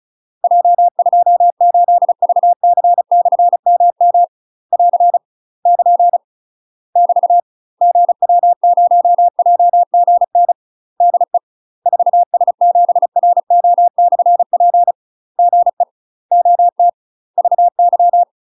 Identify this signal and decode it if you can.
Morse code, heard as a keyed tone, about 34 wpm, 700 Hz. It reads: J28VC/MM <AR> <KN> <BT> GW0JGN DE 4S7RO/P GE OT VY
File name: UK MASTER QSOs TRAINING/35WPM